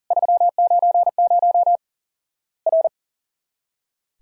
Morse code, heard as a keyed tone, about 40 wpm, 700 Hz. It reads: 390 R
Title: Worked All Europe DX Contest - QTC Exchange - 40wpm